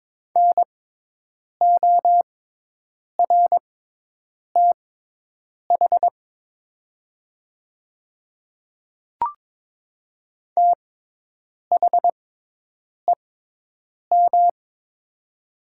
Morse code, heard as a keyed tone, about 22 wpm, spaced as for 8 wpm, 700 Hz. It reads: NORTH THEM
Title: 6x Character Spacing - Top 300 Words - 22wpm